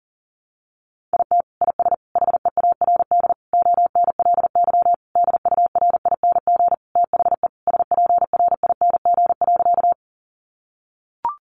Morse code, heard as a keyed tone, about 40 wpm, 700 Hz. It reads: IT IS HEARD ONLY DURING THE SPRING.
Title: Sentences from Top 1200 Words 40wpm